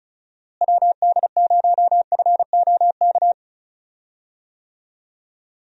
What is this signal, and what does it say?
Morse code, heard as a keyed tone, about 36 wpm, 700 Hz. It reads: WD0FOK
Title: USA Call Signs - 6 Characters - Repeated 3x - 35wpm